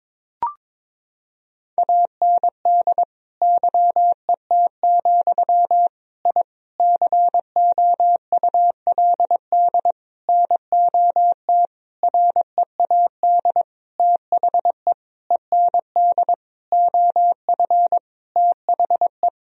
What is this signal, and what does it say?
Morse code, heard as a keyed tone, about 22 wpm, 700 Hz. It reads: AND YET, I COULD NOT READ THE END OF THE
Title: Sentences from Top 1000 Words - Review Part 1 - 22wpm